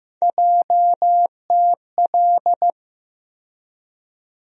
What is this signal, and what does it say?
Morse code, heard as a keyed tone, about 15 wpm, 700 Hz. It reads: JTL
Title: USA Call Sign Postfixes - 3 Letters 15wpm